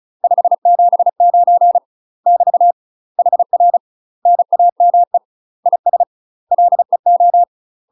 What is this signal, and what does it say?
Morse code, heard as a keyed tone, about 36 wpm, 700 Hz. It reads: 579 <BT> HR NAME IS LEO
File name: UK MASTER QSOs TRAINING / 35WPM / UK MASTER QSOs 0032